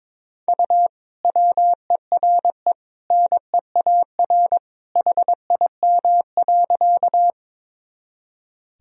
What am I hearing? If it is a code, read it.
Morse code, heard as a keyed tone, about 21 wpm, 700 Hz. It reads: U WERE NEAR HIM.
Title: Sentences from Top 200 Words and Abbreviations 22wpm